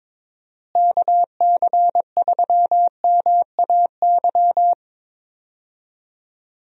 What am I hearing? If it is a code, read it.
Morse code, heard as a keyed tone, about 22 wpm, 700 Hz. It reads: KC3MAY
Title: USA Call Signs - 6 Characters - Repeated 3x - 22wpm